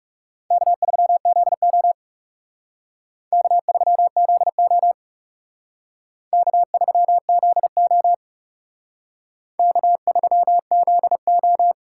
Morse code, heard as a keyed tone, about 38 wpm, 700 Hz. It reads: K3ZO K3ZO K3ZO K3ZO